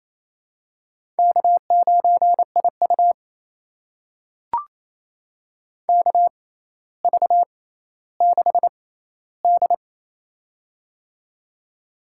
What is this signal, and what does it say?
Morse code, heard as a keyed tone, about 28 wpm, spaced as for 14 wpm, 700 Hz. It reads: K9IU KV6D
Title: USA Call Signs - 4 Characters - 6x Character Spacing - 28wpm